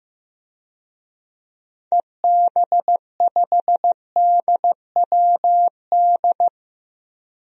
Morse code, heard as a keyed tone, about 15 wpm, 700 Hz. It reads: EB5DWD
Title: DXCC-100 Call Signs - 6 Characters - Repeated 3x - 15wpm